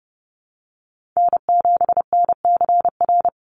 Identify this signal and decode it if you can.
Morse code, heard as a keyed tone, about 30 wpm, 700 Hz. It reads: N7NCR
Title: USA Call Signs 30wpm